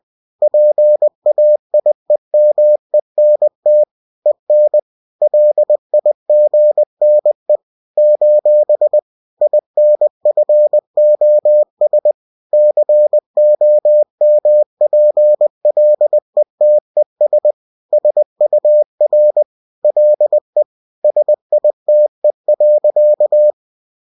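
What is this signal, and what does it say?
Morse code, heard as a keyed tone, about 20 wpm, 600 Hz. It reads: PAIEMENT EN LIGNE : INFOS COMPLETES SUR LE SITE.
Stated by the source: Bulletin sent in CW.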